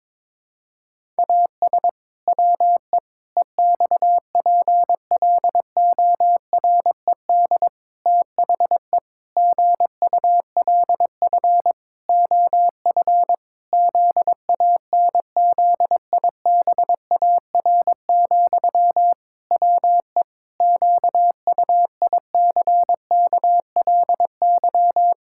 Morse code, heard as a keyed tone, about 22 wpm, 700 Hz. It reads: AS WE EXPLORED THE GULF OF ZANZIBAR, WE QUICKLY
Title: Pangram Challenge v2 - 22wpm